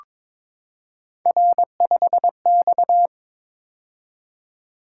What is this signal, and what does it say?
Morse code, heard as a keyed tone, about 22 wpm, 700 Hz. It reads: R5X
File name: practice / 22wpm / Three Letter-Number Combinations - 22wpm